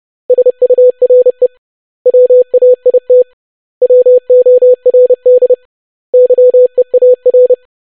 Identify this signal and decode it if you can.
Morse code, heard as a keyed tone, about 30 wpm, 500 Hz. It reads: SURE WAIT WORD YEAR
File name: basic500-25wpm